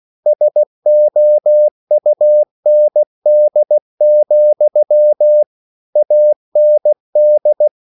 Morse code, heard as a keyed tone, about 16 wpm, 600 Hz. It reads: SOUND, AND
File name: CW / Practice / EnglishQuotations0416